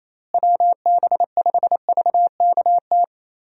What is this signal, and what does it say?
Morse code, heard as a keyed tone, about 27 wpm, 700 Hz. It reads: WB5VKT
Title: USA Call Signs - 6 Characters - Repeated 3x - 28wpm